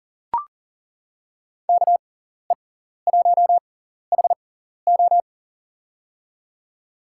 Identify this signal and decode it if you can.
Morse code, heard as a keyed tone, about 40 wpm, spaced as for 14 wpm, 700 Hz. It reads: KE1HO
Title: USA Call Signs - 5 Characters - 6x Character Spacing - 40wpm